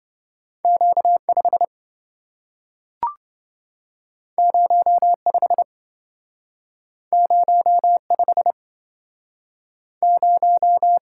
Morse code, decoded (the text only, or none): Q5 05 05 0